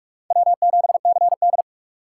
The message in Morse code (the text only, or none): W7CD